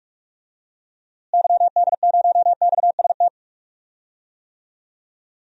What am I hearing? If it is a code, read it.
Morse code, heard as a keyed tone, about 46 wpm, 700 Hz. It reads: YD0XST
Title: DXCC-100 Call Signs - 6 Characters - Repeated 3x - 45wpm